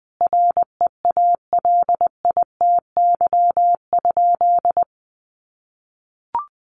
Morse code, heard as a keyed tone, about 20 wpm, 700 Hz. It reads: REALITY?
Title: Sentences from Top 1200 Words 20wpm